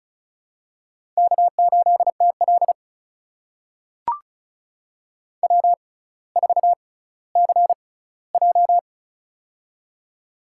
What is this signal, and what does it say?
Morse code, heard as a keyed tone, about 34 wpm, spaced as for 18 wpm, 700 Hz. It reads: K8TL W4CJ
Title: USA Call Signs - 4 Characters - 6x Character Spacing - 35wpm